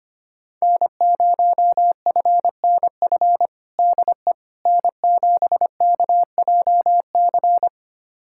Morse code, heard as a keyed tone, about 25 wpm, 700 Hz. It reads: N0FNF DE N7KJC
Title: USA Callsign DE Callsign - 5 Characters - 25wpm